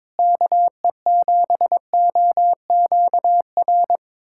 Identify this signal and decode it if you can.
Morse code, heard as a keyed tone, about 22 wpm, 700 Hz. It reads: KE7OQR
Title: USA Callsign DE Callsign - 6 Characters - 22wpm